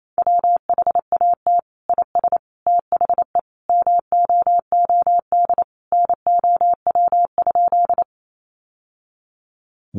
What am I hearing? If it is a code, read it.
Morse code, heard as a keyed tone, about 28 wpm, 700 Hz. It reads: WHAT IS THE MOOD NOW?
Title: Sentences from Top 1100 Words 28wpm